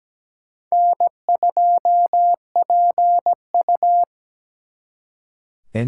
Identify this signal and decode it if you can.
Morse code, heard as a keyed tone, about 20 wpm, 700 Hz. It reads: N2PU
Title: Speed Racing - USA Call Signs 17wpm